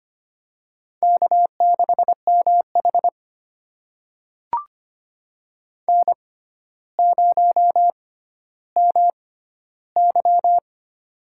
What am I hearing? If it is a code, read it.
Morse code, heard as a keyed tone, about 25 wpm, spaced as for 13 wpm, 700 Hz. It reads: K6MH N0MY
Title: USA Call Signs - 4 Characters - 6x Character Spacing - 25wpm